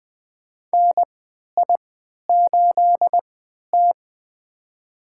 Morse code, heard as a keyed tone, about 20 wpm, spaced as for 11 wpm, 700 Hz. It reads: NI8T
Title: USA Call Signs - 4 Characters - 3x Character Spacing - 20wpm